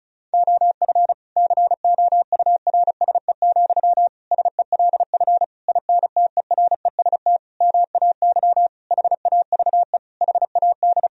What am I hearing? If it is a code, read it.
Morse code, heard as a keyed tone, about 34 wpm, 700 Hz. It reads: OF COURSE, SELF INTEREST MAY HAVE HAD
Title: Sentences from Top 1000 Words - Review Part 2 - 35wpm